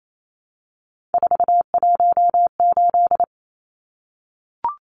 Morse code, heard as a keyed tone, about 28 wpm, 700 Hz. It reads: 418
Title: Numbers - 3-Digit 28wpm